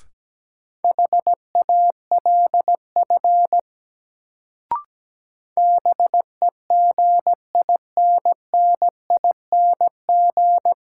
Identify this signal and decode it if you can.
Morse code, heard as a keyed tone, about 17 wpm, 700 Hz. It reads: HALF BEGINNING
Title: Sets of 3 Words - New Word Drill - 17wpm